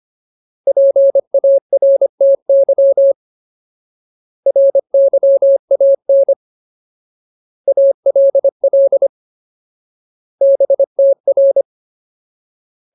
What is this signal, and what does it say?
Morse code, heard as a keyed tone, about 25 wpm, 550 Hz. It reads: PARTY RYAN ALL BTR
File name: Ultimate Word List / 25WPM Ultimate